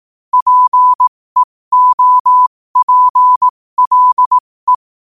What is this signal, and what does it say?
Morse code, heard as a keyed tone, about 18 wpm, spaced as for 15 wpm, 1000 Hz. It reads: PEOPLE
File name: qotdfiles / 15wpm